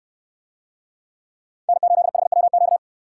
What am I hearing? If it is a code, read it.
Morse code, heard as a keyed tone, about 68 wpm, 700 Hz. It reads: N8HWQ